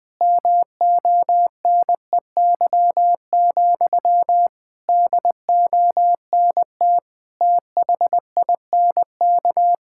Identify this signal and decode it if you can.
Morse code, heard as a keyed tone, about 20 wpm, 700 Hz. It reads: MONEY, DONT THINK
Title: Sentences from Top 1000 Words - Review Part 5 - 20wpm